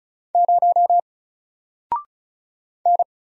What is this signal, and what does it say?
Morse code, heard as a keyed tone, about 33 wpm, 700 Hz. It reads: 0 N
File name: Single-Letter-Number-35wpm